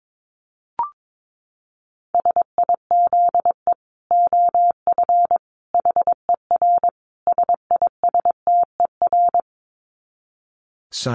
Morse code, heard as a keyed tone, about 22 wpm, 700 Hz. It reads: SIZE OF HER SISTER
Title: Dit Challenge 22wpm